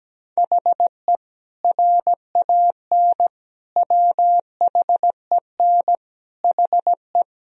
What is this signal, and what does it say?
Morse code, heard as a keyed tone, about 17 wpm, 700 Hz. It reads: HE RAN WHEN HE
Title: Sentences from Top 500 Words and Abbreviations 17wpm